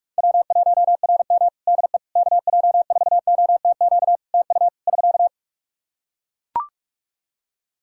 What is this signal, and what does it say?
Morse code, heard as a keyed tone, about 41 wpm, 700 Hz. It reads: W1RM DE KJ4OTQ TU <SK>
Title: Practice QSO - Exchange 4 - Ending - 45wpm